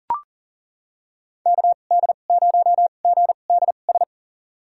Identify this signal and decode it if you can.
Morse code, heard as a keyed tone, about 38 wpm, spaced as for 28 wpm, 700 Hz. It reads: KD0GDS
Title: USA Call Signs - 6 Characters - 2x Character Spacing - 40wpm